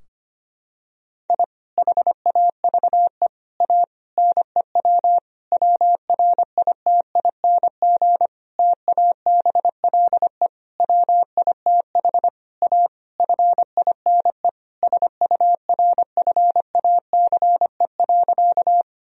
Morse code, heard as a keyed tone, about 25 wpm, 700 Hz. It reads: I HAVE A NEW WRITING TABLE WITH A FINE SURFACE.
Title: Sentences from Top 1000 Words - Review Part 3 - 25wpm